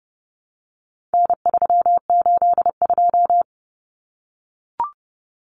Morse code, heard as a keyed tone, about 30 wpm, 700 Hz. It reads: N382
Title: Common Cut Numbers - 4-Digit 30wpm